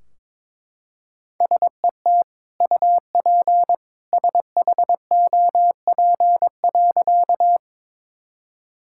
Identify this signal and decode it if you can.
Morse code, heard as a keyed tone, about 22 wpm, 700 Hz. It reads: SET UP SHOP.
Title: Sentences from Top 1600 Words 22wpm